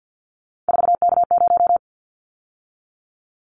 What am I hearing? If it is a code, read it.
Morse code, heard as a keyed tone, about 50 wpm, 700 Hz. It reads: 4K0
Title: Callsign Prefixes for Top 100 DXCC - 50wpm